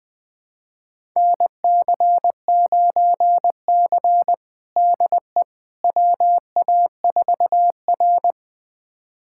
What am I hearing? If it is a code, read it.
Morse code, heard as a keyed tone, about 21 wpm, 700 Hz. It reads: NC9C DE WA4R
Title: USA Callsign DE Callsign - 4 Characters - 20wpm